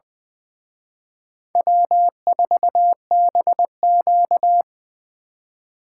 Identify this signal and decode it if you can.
Morse code, heard as a keyed tone, about 24 wpm, 700 Hz. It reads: W4BQ
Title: Speed Racing - USA 1x2 Call Signs 20wpm